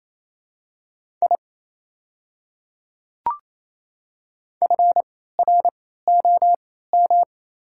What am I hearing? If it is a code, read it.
Morse code, heard as a keyed tone, about 28 wpm, spaced as for 16 wpm, 700 Hz. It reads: I FROM
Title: 3x Character Spacing - Top 300 Words - 28wpm